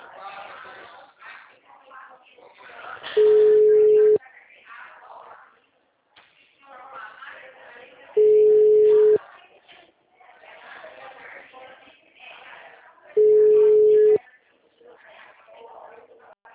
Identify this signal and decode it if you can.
Ringback tone (Europe, 425 Hz), with speech.